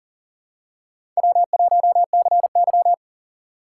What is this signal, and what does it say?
Morse code, heard as a keyed tone, about 40 wpm, 700 Hz. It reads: W1CY
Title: USA Call Signs - 4 Characters - 3x Character Spacing - 40wpm